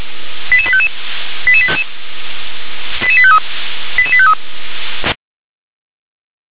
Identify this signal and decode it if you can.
Five-tone selective call (ZVEI), heard as quick runs of five tones, tones of about 70 ms. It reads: 88855 88853 88853